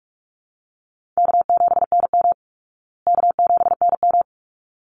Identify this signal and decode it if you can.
Morse code, heard as a keyed tone, about 44 wpm, 700 Hz. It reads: K7NM K7NM
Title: Call Signs Repeated 3x - 45wpm